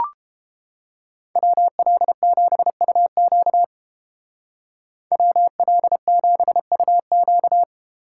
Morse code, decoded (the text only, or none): WL7UQ WL7UQ